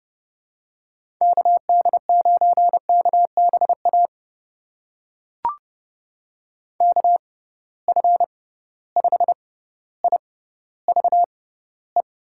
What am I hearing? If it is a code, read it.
Morse code, heard as a keyed tone, about 31 wpm, spaced as for 15 wpm, 700 Hz. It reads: KD9KBA KF5IVE
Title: USA Call Signs - 6 Characters - 6x Character Spacing - 30wpm